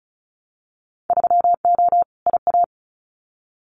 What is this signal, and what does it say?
Morse code, heard as a keyed tone, about 36 wpm, 690 Hz. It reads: 3O IA
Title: Winter Field Day Exchanges 35wpm